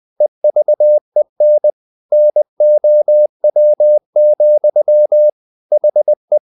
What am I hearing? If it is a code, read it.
Morse code, heard as a keyed tone, about 20 wpm, 600 Hz. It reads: EVEN NOW, HE
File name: the old man and the sea part 2 20wpm